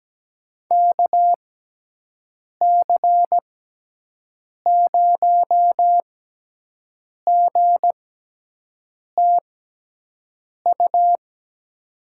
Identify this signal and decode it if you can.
Morse code, heard as a keyed tone, about 17 wpm, 700 Hz. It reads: K C 0 G T U
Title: USA Call Signs - 6 Characters - 6x Character Spacing - 17wpm